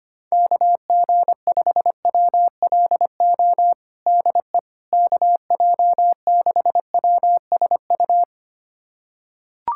Morse code, heard as a keyed tone, about 25 wpm, 700 Hz. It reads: KG5WLO DE KJ6WSU
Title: USA Callsign DE Callsign 25wpm